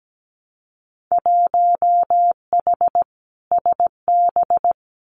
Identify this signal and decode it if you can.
Morse code, heard as a keyed tone, about 17 wpm, 700 Hz. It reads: 1H SB
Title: Winter Field Day Exchanges 17wpm